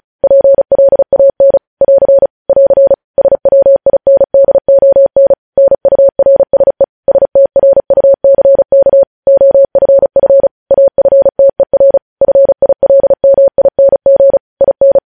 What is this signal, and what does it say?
Morse code, heard as a keyed tone, about 36 wpm, 550 Hz. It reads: PLAN <AR> <AR> SWINDON NURSE STRUCK OFF AFTER FILMING IN
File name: podcast / News headlines-1x-35x35